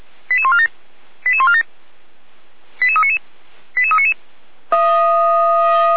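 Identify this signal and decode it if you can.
Five-tone selective call (ZVEI), heard as quick runs of five tones, tones of about 70 ms. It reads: 88147 88147 88399 88399
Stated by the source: ZVEI five-tone selective call received on an SDR.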